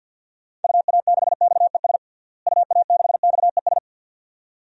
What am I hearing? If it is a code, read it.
Morse code, heard as a keyed tone, about 49 wpm, 700 Hz. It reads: UA6XES UA6XES
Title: DXCC-100 Call Signs - 6 Characters - Repeated 2x - 50wpm